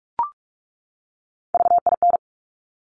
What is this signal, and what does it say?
Morse code, heard as a keyed tone, about 46 wpm, 690 Hz. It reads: VIN
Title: Top Three Letter Combinations in Words 45wpm